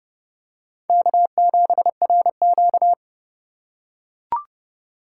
Morse code, heard as a keyed tone, about 28 wpm, 700 Hz. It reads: K7RQ